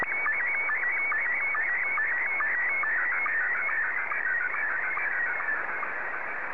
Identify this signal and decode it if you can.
SSTV (slow-scan television), Scottie 1.